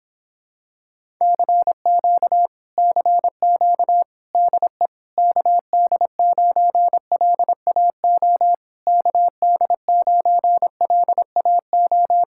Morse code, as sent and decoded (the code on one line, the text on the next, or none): -.-. --.- / -.-. --.- / -.. / -.- -.. ----. .-.. .- --- / -.- -.. ----. .-.. .- ---
CQ CQ DE KD9LAO KD9LAO